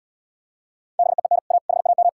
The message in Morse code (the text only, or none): BEEN A HAM